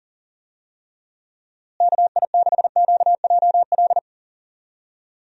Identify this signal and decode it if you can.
Morse code, heard as a keyed tone, about 40 wpm, 700 Hz. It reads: KI6QJL
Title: USA Call Signs - 6 Characters - Repeated 2x - 40wpm